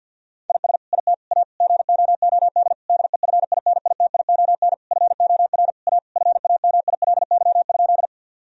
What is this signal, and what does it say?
Morse code, heard as a keyed tone, about 50 wpm, 700 Hz. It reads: IS IT A GOOD DEFINITION FOR A FAMILY?
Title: Sentences from Top 1000 Words - Review Part 4 - 50wpm